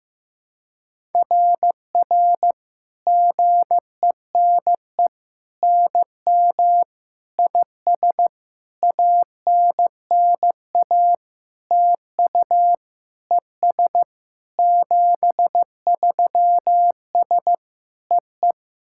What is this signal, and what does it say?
Morse code, heard as a keyed tone, about 15 wpm, 700 Hz. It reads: RR GENE NM IS ANNA TU ES 73S EE
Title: SOTA QSO Practice 15wpm